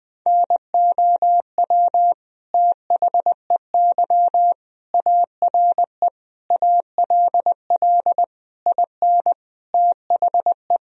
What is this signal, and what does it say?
Morse code, heard as a keyed tone, about 20 wpm, 700 Hz. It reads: NOW THEY ARE ALL IN THE
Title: Sentences from Top 1000 Words - Review Part 2 - 20wpm